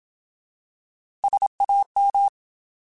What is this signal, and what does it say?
Morse code, heard as a keyed tone, about 27 wpm, 800 Hz. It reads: SAM
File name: Alphabetical / Alphabetical mp3s / Ss @25